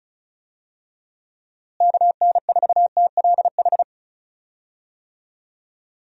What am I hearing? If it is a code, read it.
Morse code, heard as a keyed tone, about 36 wpm, 700 Hz. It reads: KN4TLH
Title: USA Call Signs - 6 Characters - Repeated 3x - 35wpm